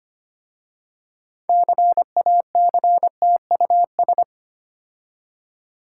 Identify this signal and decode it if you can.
Morse code, heard as a keyed tone, about 25 wpm, 700 Hz. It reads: CACTUS
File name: practice / 25wpm / Difficult Words with Hints - 25wpm